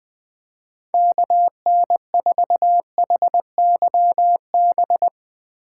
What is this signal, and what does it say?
Morse code, heard as a keyed tone, about 20 wpm, 700 Hz. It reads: KN4HYB
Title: Speed Racing - USA 2x3 Call Signs 17wpm